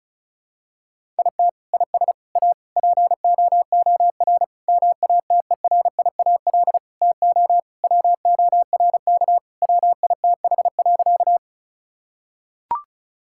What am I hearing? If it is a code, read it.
Morse code, heard as a keyed tone, about 36 wpm, 700 Hz. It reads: IT IS A POOR MATERIAL TO WORK WITH.
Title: Sentences from Top 1000 Words - Review Part 5 - 35wpm